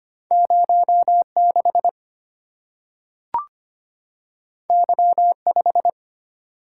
Morse code, heard as a keyed tone, about 25 wpm, 700 Hz. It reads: 0 6 Y 5
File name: Two Letter-Number Combinations - 25wpm